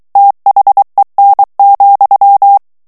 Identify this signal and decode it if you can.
Morse code, heard as a keyed tone, about 23 wpm, 800 Hz. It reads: THEN,
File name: ANGLAIS texte clair 3 vitesse 22 mots minute